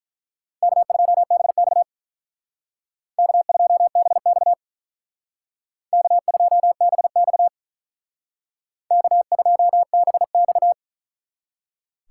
Morse code, heard as a keyed tone, about 43 wpm, 700 Hz. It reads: K2BX K2BX K2BX K2BX
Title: Speed Racing - USA 1x2 Call Signs 35wpm